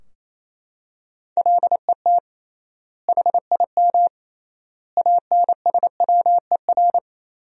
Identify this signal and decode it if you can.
Morse code, heard as a keyed tone, about 28 wpm, 700 Hz. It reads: LET HIM ANSWER
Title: 3x Word-Spacing - Sentences from Top 300 Words - 28wpm